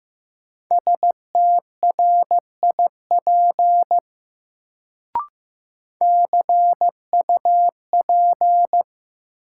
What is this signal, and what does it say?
Morse code, heard as a keyed tone, about 15 wpm, 700 Hz. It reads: STRIP CUP
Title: Sentences from Top 1600 Words - New Word Drill - 15wpm